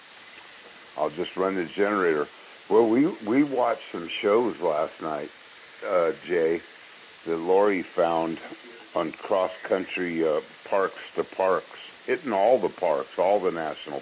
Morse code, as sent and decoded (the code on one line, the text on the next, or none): none